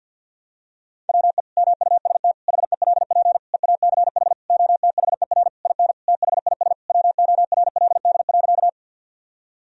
Morse code, heard as a keyed tone, about 53 wpm, 700 Hz. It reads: WE MUST HELP EACH OTHER IN THIS WORLD.
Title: Sentences from Top 1000 Words - Review Part 1 - 50wpm